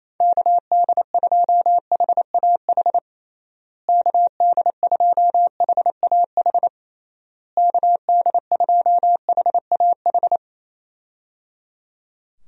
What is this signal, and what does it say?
Morse code, heard as a keyed tone, about 27 wpm, 700 Hz. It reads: KD2HAH KD2HAH KD2HAH
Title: USA Call Signs - 6 Characters - Repeated 3x - 28wpm